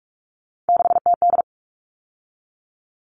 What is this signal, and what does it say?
Morse code, heard as a keyed tone, about 44 wpm, 700 Hz. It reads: 6TD
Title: All Cut Numbers - 3-Digit 45wpm